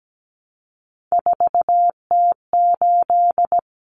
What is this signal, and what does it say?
Morse code, heard as a keyed tone, about 17 wpm, 690 Hz. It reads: 4T8
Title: Common Cut Numbers - 3-Digit 17wpm